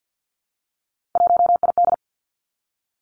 Morse code, heard as a keyed tone, about 50 wpm, 700 Hz. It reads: 1ID
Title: Three Challenging Letter-Number Combinations 50wpm